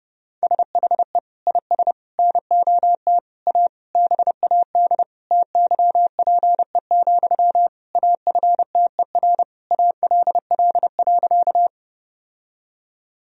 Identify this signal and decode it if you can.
Morse code, heard as a keyed tone, about 30 wpm, 700 Hz. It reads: SHE IS NOT A BAD TYPE, AFTER ALL.
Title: Sentences from Top 1000 Words - Review Part 3 - 30wpm